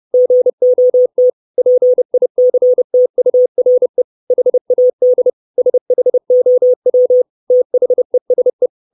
Morse code, heard as a keyed tone, about 30 wpm, 500 Hz. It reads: GOT PICTURE HAD SHOW THESE
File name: 632 Words 30WPM